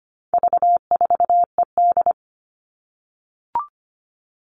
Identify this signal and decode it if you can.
Morse code, heard as a keyed tone, about 25 wpm, 700 Hz. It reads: V4ED
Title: All Cut Numbers - 4-Digit 25wpm